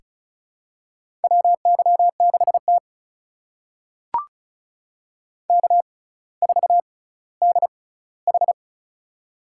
Morse code, heard as a keyed tone, about 36 wpm, spaced as for 18 wpm, 700 Hz. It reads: WY6T K4DH
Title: USA Call Signs - 4 Characters - 6x Character Spacing - 35wpm